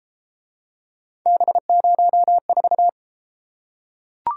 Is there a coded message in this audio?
Yes: Morse code from 1.3 to 2.9 s, heard as a keyed tone, about 32 wpm, 700 Hz. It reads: B04